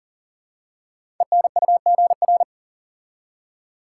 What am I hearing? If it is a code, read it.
Morse code, heard as a keyed tone, about 43 wpm, 700 Hz. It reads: ENUGR
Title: Code Groups - 5 Letters - 40wpm